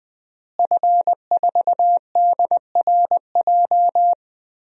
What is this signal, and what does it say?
Morse code, heard as a keyed tone, about 21 wpm, 700 Hz. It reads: F4DRJ
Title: DXCC-100 Call Signs - 5 Characters - Repeated 3x - 20wpm